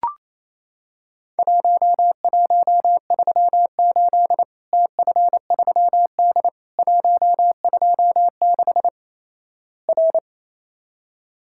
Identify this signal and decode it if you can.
Morse code, heard as a keyed tone, about 28 wpm, 700 Hz. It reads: 1138 TF3D 126 R
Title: Worked All Europe DX Contest - QTC Exchange - 28wpm